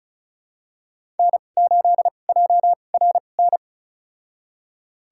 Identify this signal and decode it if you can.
Morse code, heard as a keyed tone, about 35 wpm, spaced as for 25 wpm, 690 Hz. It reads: N8JRN
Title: USA Call Signs - 5 Characters - 2x Character Spacing - 35wpm